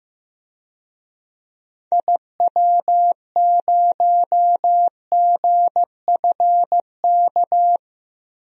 Morse code, heard as a keyed tone, about 15 wpm, 700 Hz. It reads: IW0GFK
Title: DXCC-100 Call Signs - 6 Characters - Repeated 2x - 15wpm